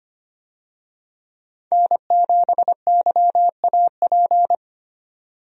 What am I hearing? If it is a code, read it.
Morse code, heard as a keyed tone, about 25 wpm, 690 Hz. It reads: N7YAP
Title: USA Call Signs - 5 Characters - 25wpm